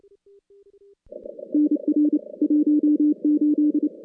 Morse code, heard as a keyed tone, about 29 wpm, 300 Hz. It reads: NR 18
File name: contests / CW-SS-k5tr- -2100